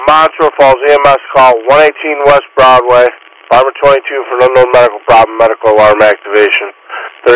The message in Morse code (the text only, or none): none